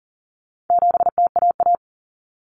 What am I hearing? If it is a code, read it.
Morse code, heard as a keyed tone, about 40 wpm, 700 Hz. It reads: 7TAA
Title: Common Cut Numbers - 4-Digit 40wpm